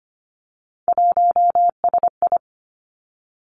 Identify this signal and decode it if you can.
Morse code, heard as a keyed tone, about 25 wpm, 700 Hz. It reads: 1SI
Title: Three Challenging Letter-Number Combinations 25wpm